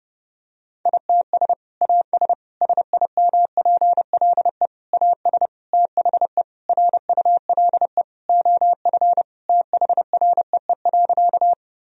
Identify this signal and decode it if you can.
Morse code, heard as a keyed tone, about 30 wpm, 700 Hz. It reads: ITS AS SIMPLE AS THE RULE OF THREE.
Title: Sentences from Top 1000 Words - Review Part 2 - 30wpm